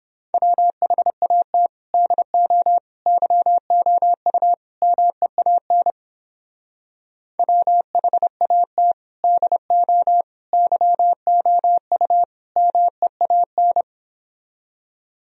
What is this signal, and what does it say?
Morse code, heard as a keyed tone, about 29 wpm, 700 Hz. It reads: WHAT DO YOU MEAN WHAT DO YOU MEAN